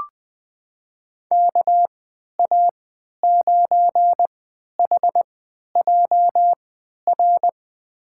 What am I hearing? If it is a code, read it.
Morse code, heard as a keyed tone, about 20 wpm, spaced as for 11 wpm, 700 Hz. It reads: KA9HJR
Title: USA Call Signs - 6 Characters - 3x Character Spacing - 20wpm